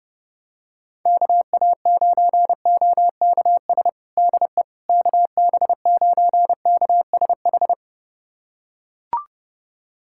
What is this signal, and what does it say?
Morse code, heard as a keyed tone, about 30 wpm, 700 Hz. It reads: KA9OKS DE KB9KSH
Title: USA Callsign DE Callsign - 6 Characters - 30wpm